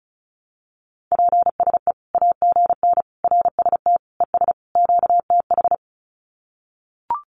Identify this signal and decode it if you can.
Morse code, heard as a keyed tone, about 34 wpm, 700 Hz. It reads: PSE AGN RST ES QTH
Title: Common QSO Phrases 35wpm